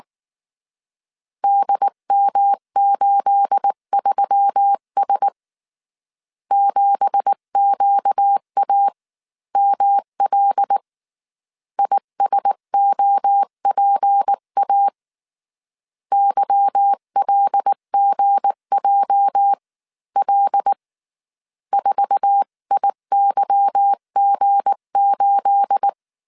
Morse code, heard as a keyed tone, about 19 wpm, spaced as for 16 wpm, 800 Hz. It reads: DM83S 7QAML ISOPA YLGJL 4IYG8
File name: CW / DET / 1 9 1500C S B 18wpm 2551